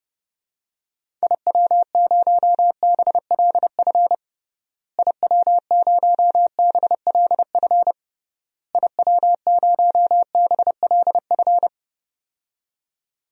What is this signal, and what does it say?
Morse code, heard as a keyed tone, about 30 wpm, 700 Hz. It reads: IW0BLF IW0BLF IW0BLF
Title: DXCC-100 Call Signs - 6 Characters - Repeated 3x - 30wpm